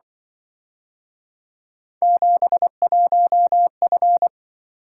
Morse code, heard as a keyed tone, about 24 wpm, 700 Hz. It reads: 71F
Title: Speed-Racing - Three Letter-Number Combinations - 20wpm